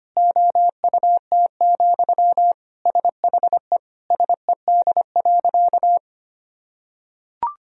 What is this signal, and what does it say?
Morse code, heard as a keyed tone, about 25 wpm, 700 Hz. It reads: OUT, SHE SED.
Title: Sentences from Top 500 Words and Abbreviations 25wpm